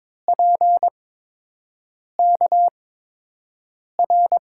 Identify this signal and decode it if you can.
Morse code, heard as a keyed tone, about 21 wpm, spaced as for 6 wpm, 700 Hz. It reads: PKR
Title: USA Call Signs - 6 Characters - 8x Character Spacing - 22wpm